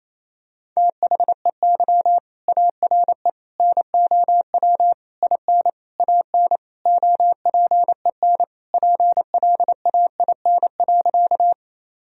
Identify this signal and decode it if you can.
Morse code, heard as a keyed tone, about 28 wpm, 700 Hz. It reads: THEY ARE NOW IN AN OPEN PLAIN.
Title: Sentences from Top 1000 Words - Review Part 2 - 28wpm